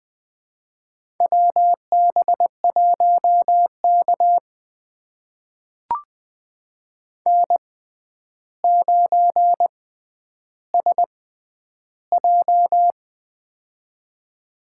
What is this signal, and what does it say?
Morse code, heard as a keyed tone, about 20 wpm, spaced as for 10 wpm, 700 Hz. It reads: WB1K N9SJ
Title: USA Call Signs - 4 Characters - 6x Character Spacing - 20wpm